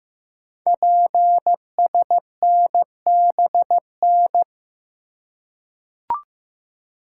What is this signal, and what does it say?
Morse code, heard as a keyed tone, about 15 wpm, 700 Hz. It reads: PSNBN
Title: Code Groups - 5 Letters - 15wpm